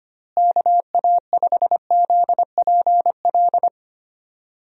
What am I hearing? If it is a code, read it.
Morse code, heard as a keyed tone, about 25 wpm, 700 Hz. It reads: KA5ZPL
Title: USA Call Signs - 6 Characters - Repeated 2x - 25wpm